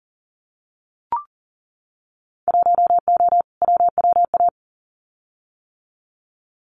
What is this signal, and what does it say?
Morse code, heard as a keyed tone, about 37 wpm, 700 Hz. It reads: 1O WWA
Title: Winter Field Day Exchanges 40wpm